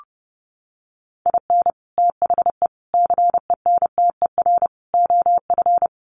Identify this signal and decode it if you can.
Morse code, heard as a keyed tone, about 30 wpm, 700 Hz. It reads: IN THE CENTER OF
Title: Sets of 4 Words 30wpm